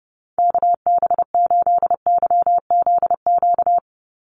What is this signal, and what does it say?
Morse code, heard as a keyed tone, about 30 wpm, 700 Hz. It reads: KB8YZQ
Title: USA Call Signs 30wpm